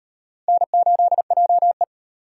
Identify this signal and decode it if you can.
Morse code, heard as a keyed tone, about 33 wpm, 700 Hz. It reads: N8JE